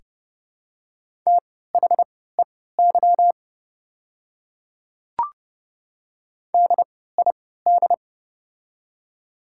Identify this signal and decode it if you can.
Morse code, heard as a keyed tone, about 29 wpm, spaced as for 17 wpm, 700 Hz. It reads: THEY DID
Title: 3x Character Spacing - Top 200 Words - 30wpm